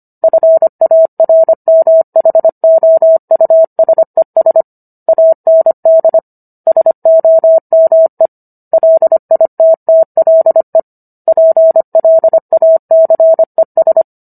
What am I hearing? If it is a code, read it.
Morse code, heard as a keyed tone, about 25 wpm, 650 Hz. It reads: FARMHOUSES AND SOME LITTLE PLACES